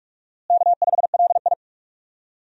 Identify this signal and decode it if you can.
Morse code, heard as a keyed tone, about 48 wpm, 700 Hz. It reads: K5LI
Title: USA Call Signs - 4 Characters - Repeated 2x - 45wpm